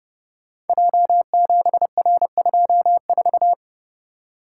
Morse code, heard as a keyed tone, about 30 wpm, 700 Hz. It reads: J7R24
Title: Code Groups - 5 Letter:Numbers - 30wpm